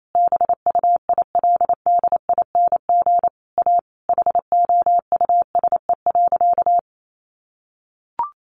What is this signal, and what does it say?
Morse code, heard as a keyed tone, about 28 wpm, 700 Hz. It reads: BUILDING A HOUSE.
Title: Sentences from Top 700 Words 28wpm